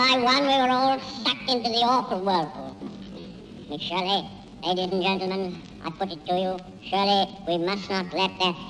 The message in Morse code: none